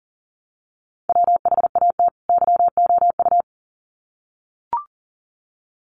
Morse code, heard as a keyed tone, about 40 wpm, 700 Hz. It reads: WHAT YOU
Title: Sets of 2 Words 40wpm